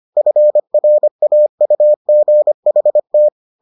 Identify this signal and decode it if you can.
Morse code, heard as a keyed tone, about 25 wpm, 600 Hz. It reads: FRAUGHT